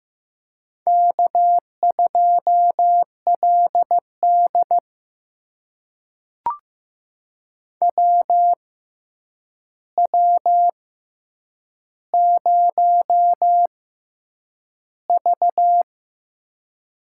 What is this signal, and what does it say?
Morse code, heard as a keyed tone, about 15 wpm, spaced as for 8 wpm, 700 Hz. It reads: K2LD WW0V
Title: USA Call Signs - 4 Characters - 6x Character Spacing - 15wpm